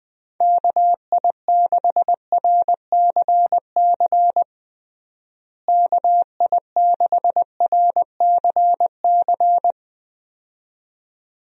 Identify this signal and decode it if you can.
Morse code, heard as a keyed tone, about 20 wpm, 700 Hz. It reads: KI6RCC KI6RCC
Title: USA Call Signs - 6 Characters - Repeated 2x - 20wpm